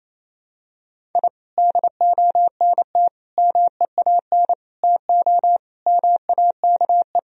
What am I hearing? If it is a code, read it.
Morse code, heard as a keyed tone, about 28 wpm, 700 Hz. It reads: I DONT MEAN TO MAKE
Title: Sentences from Top 1000 Words - Review Part 1 - 28wpm